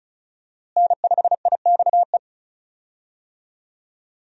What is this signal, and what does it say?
Morse code, heard as a keyed tone, about 34 wpm, 700 Hz. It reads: N5IXE
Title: USA Call Signs - 5 Characters - Repeated 3x - 35wpm